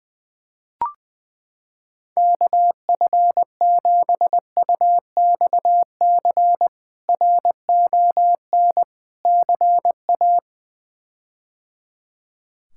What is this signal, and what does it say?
Morse code, heard as a keyed tone, about 20 wpm, 700 Hz. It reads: KF7UXC RON CA
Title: K1USN SST QSO Practice - 20wpm